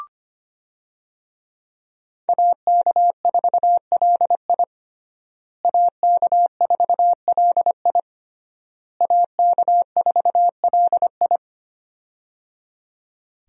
Morse code, heard as a keyed tone, about 25 wpm, 700 Hz. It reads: AK4LI AK4LI AK4LI
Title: USA Call Signs - 5 Characters - Repeated 3x - 25wpm